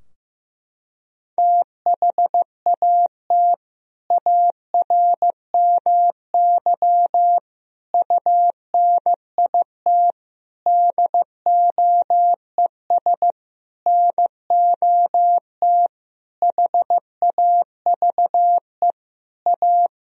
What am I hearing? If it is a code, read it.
Morse code, heard as a keyed tone, about 15 wpm, 700 Hz. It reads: THAT ARMY UNIT DOES NOT HAVE A
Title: Sentences from Top 1000 Words - Review Part 2 - 15wpm